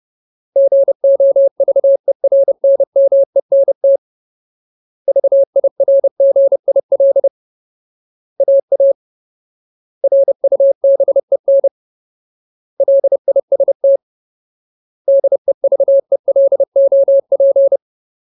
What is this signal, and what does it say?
Morse code, heard as a keyed tone, about 30 wpm, 550 Hz. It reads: GOVERNMENT VIRGIL AA RUBEN LIST DEVELOP
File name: Ultimate Word List / 30WPM Ultimate